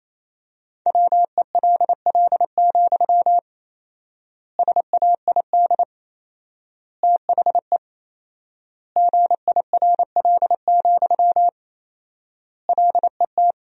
Morse code, heard as a keyed tone, about 28 wpm, 700 Hz. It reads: WELL, SAID THE GIRL, LET
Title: 4x Word-Spacing - Sentences from Top 400 Words - 28wpm